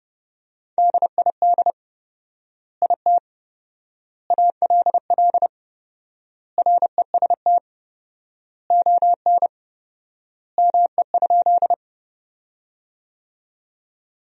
Morse code, heard as a keyed tone, about 30 wpm, 700 Hz. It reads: DID IT ALL REST ON ME?
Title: 4x Word-Spacing - Sentences from Top 500 Words - 30wpm